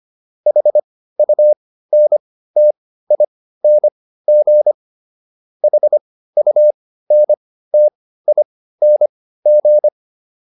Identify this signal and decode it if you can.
Morse code, heard as a keyed tone, about 25 wpm, spaced as for 15 wpm, 600 Hz. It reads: HUNTING HUNTING